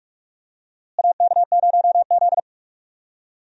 Morse code, heard as a keyed tone, about 48 wpm, 700 Hz. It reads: AK0Z
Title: USA Call Signs - 4 Characters - Repeated 2x - 45wpm